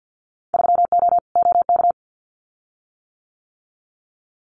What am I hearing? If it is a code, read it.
Morse code, heard as a keyed tone, about 50 wpm, 700 Hz. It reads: 3O OK
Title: Winter Field Day Exchanges Encore 50wpm